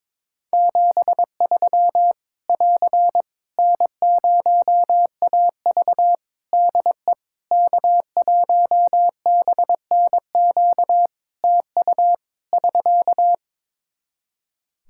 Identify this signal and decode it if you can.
Morse code, heard as a keyed tone, about 22 wpm, 700 Hz. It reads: 73 <AR> N0AV DE K1BNQ TU <SK>
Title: Practice QSO - Exchange 4 - Ending - 22wpm